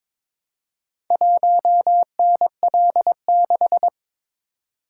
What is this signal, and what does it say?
Morse code, heard as a keyed tone, about 22 wpm, 700 Hz. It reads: 1NL6
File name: practice / 22wpm / Four Letter-Number Combinations - 22wpm